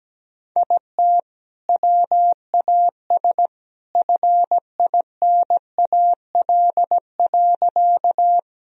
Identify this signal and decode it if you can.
Morse code, heard as a keyed tone, about 17 wpm, 700 Hz. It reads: IT WAS FINAL.
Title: Sentences from Top 1000 Words - Review Part 2 - 17wpm